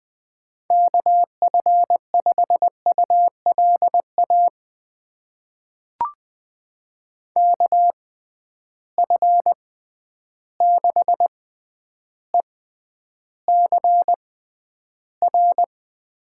Morse code, heard as a keyed tone, about 20 wpm, spaced as for 10 wpm, 700 Hz. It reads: KF5ULA KF6ECR
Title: USA Call Signs - 6 Characters - 6x Character Spacing - 20wpm